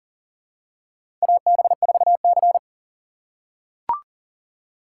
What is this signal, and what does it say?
Morse code, heard as a keyed tone, about 40 wpm, 700 Hz. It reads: AB4C
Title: USA Call Signs - 3x Character Spacing - 40wpm